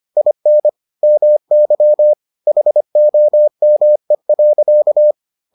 Morse code, heard as a keyed tone, about 25 wpm, 600 Hz. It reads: IN MY HOME.